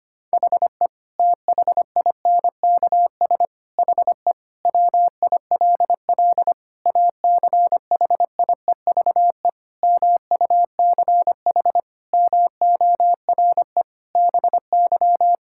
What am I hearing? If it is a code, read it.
Morse code, heard as a keyed tone, about 25 wpm, 700 Hz. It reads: HE THINKS HE WILL ACHIEVE MUCH MORE BY
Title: Sentences from Top 1000 Words - Review Part 6 - 25wpm